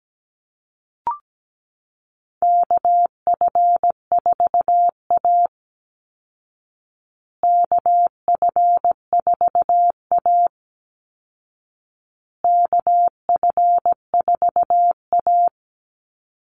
Morse code, heard as a keyed tone, about 17 wpm, 700 Hz. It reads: KF4A KF4A KF4A
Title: Call Signs Repeated 2x - 17wpm